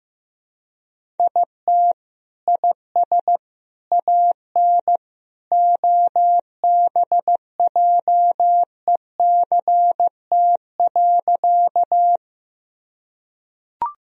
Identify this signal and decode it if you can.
Morse code, heard as a keyed tone, about 15 wpm, 700 Hz. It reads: IT IS AN OBJECT.
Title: Sentences from Top 1000 Words - Review Part 3 - 15wpm